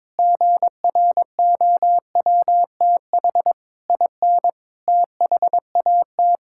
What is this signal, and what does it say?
Morse code, heard as a keyed tone, about 22 wpm, 700 Hz. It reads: GROWTH IN THAT
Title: Sentences from Top 1000 Words - Review Part 5 - 22wpm